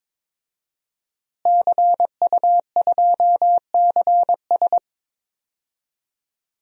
Morse code, heard as a keyed tone, about 22 wpm, 700 Hz. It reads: CU2CS
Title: DXCC-100 Call Signs - 5 Characters - Repeated 3x - 22wpm